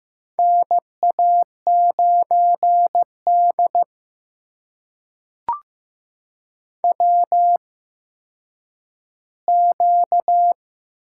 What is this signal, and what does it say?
Morse code, heard as a keyed tone, about 15 wpm, spaced as for 6 wpm, 700 Hz. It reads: NA9DWQ
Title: USA Call Signs - 8x Character Spacing - 15wpm